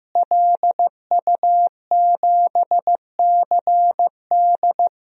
Morse code, heard as a keyed tone, about 15 wpm, 700 Hz. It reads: LU7CD
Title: DXCC-100 Call Signs - 5 Characters - Repeated 3x - 15wpm